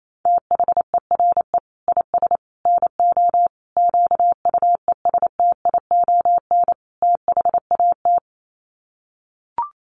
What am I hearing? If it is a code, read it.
Morse code, heard as a keyed tone, about 28 wpm, 700 Hz. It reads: THERE IS NO QUESTION THAT
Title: Sets of 5 Words Encore 28wpm